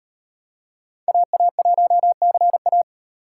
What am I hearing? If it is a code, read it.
Morse code, heard as a keyed tone, about 33 wpm, 700 Hz. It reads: AA1CA